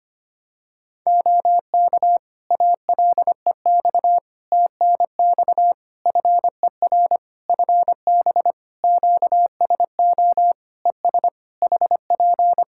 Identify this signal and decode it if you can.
Morse code, heard as a keyed tone, about 25 wpm, 700 Hz. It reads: OK ALEX TNX FER FB QSO ES HP
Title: Practice QSO - Exchange 4 - Ending - 25wpm